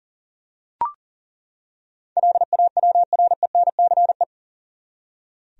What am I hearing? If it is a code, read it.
Morse code, heard as a keyed tone, about 40 wpm, 700 Hz. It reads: LAWRENCE
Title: USA Common Names - 40wpm